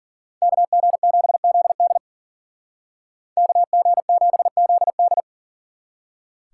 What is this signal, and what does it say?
Morse code, heard as a keyed tone, about 49 wpm, 700 Hz. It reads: KG7ZD KG7ZD
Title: Speed Racing - USA Call Signs 40wpm